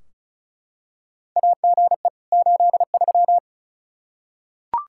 Morse code, heard as a keyed tone, about 34 wpm, 700 Hz. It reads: AGE 83
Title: K1USN SST QSO Practice 35wpm